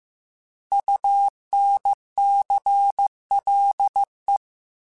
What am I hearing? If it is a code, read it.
Morse code, heard as a keyed tone, about 15 wpm, 800 Hz. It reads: UNCLE
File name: Alphabetical / Alphabetical mp3s / Us @13